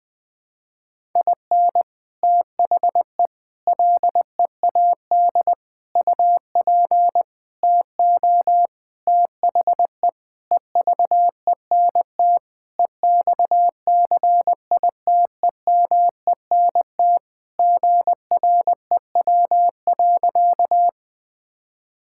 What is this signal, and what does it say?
Morse code, heard as a keyed tone, about 19 wpm, 700 Hz. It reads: IN THE LEAD UP TO THE EVENT EXCITEMENT GREW.
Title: Sentences with 6-Word Phrases - 20wpm